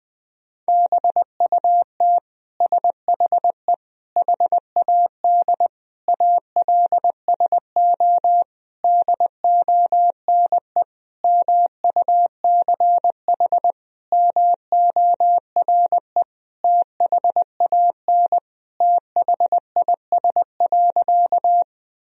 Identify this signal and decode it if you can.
Morse code, heard as a keyed tone, about 20 wpm, 700 Hz. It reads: BUT SHE HAD ALSO DONE MUCH MORE THAN THIS.
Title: Sentences from Top 1000 Words - Review Part 1 - 20wpm